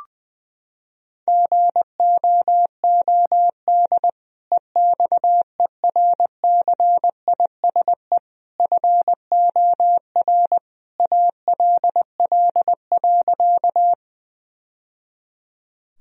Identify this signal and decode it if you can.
Morse code, heard as a keyed tone, about 20 wpm, 700 Hz. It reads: GOOD EXERCISE FOR ALL.
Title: Sentences from Top 1000 Words - Review Part 4 - 20wpm